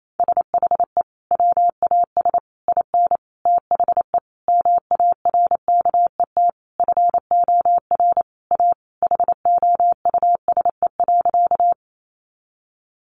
Morse code, heard as a keyed tone, about 28 wpm, 700 Hz. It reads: SHE WAS IN THE MARKET FOR A HOUSE.
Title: Sentences from Top 600 Words 28wpm